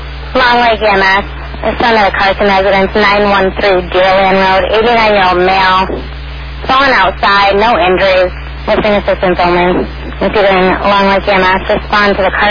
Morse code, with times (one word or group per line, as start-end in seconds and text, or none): none